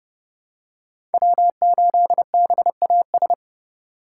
Morse code, heard as a keyed tone, about 30 wpm, 700 Hz. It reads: W8BAS
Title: USA Call Signs - 2x Character Spacing - 30wpm